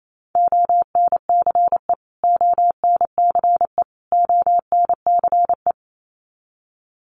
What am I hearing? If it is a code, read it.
Morse code, heard as a keyed tone, about 28 wpm, 700 Hz. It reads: ONCE ONCE ONCE
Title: Top 300 Words Repeated 28wpm